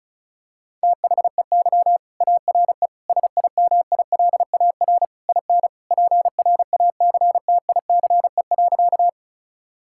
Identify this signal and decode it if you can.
Morse code, heard as a keyed tone, about 36 wpm, 700 Hz. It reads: THEY ARE SIMILAR IN PRACTICE.
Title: Sentences from Top 1000 Words - Review Part 5 - 35wpm